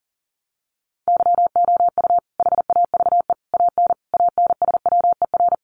Morse code, heard as a keyed tone, about 40 wpm, 700 Hz. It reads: YOU HAVE AN ANSWER
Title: Sentences from Top 700 Words 40wpm